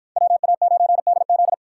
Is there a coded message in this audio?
Yes: Morse code from 0.2 to 1.5 s, heard as a keyed tone, about 53 wpm, 700 Hz. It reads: WA9DZ